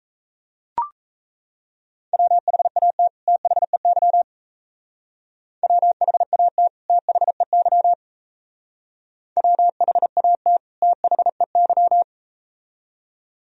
Morse code, heard as a keyed tone, about 38 wpm, 700 Hz. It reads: WHAT THEY WHAT THEY WHAT THEY